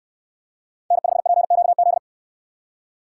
Morse code, heard as a keyed tone, about 69 wpm, 700 Hz. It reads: N5PQZ